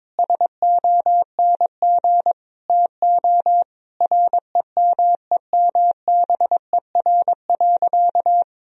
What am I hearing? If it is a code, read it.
Morse code, heard as a keyed tone, about 22 wpm, 700 Hz. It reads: SONG TO REMEMBER.
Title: Sentences from Top 1000 Words - Review Part 2 - 22wpm